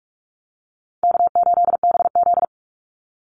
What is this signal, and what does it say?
Morse code, heard as a keyed tone, about 44 wpm, 700 Hz. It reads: K8BZ
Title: Easy Call Signs Repeated 2x - 45wpm